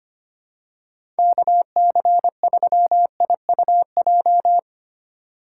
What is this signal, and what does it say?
Morse code, heard as a keyed tone, about 25 wpm, 700 Hz. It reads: KC3IUJ
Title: USA Call Signs - 2x Character Spacing - 25wpm